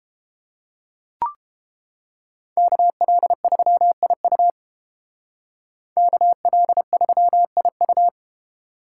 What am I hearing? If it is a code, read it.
Morse code, heard as a keyed tone, about 30 wpm, 700 Hz. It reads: KL3IU KL3IU